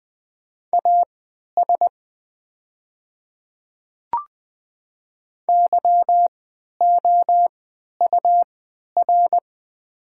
Morse code, heard as a keyed tone, about 20 wpm, spaced as for 11 wpm, 700 Hz. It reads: AS YOUR
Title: 3x Character Spacing - Top 100 Words - 20wpm